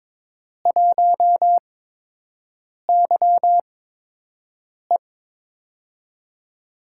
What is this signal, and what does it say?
Morse code, heard as a keyed tone, about 22 wpm, spaced as for 6 wpm, 700 Hz. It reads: 1YE
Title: USA Call Signs - 4 Characters - 8x Character Spacing - 22wpm